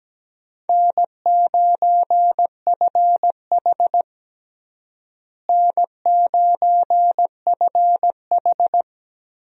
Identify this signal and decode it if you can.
Morse code, heard as a keyed tone, about 17 wpm, 700 Hz. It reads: N9FH N9FH
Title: USA Call Signs - 4 Characters - Repeated 3x - 17wpm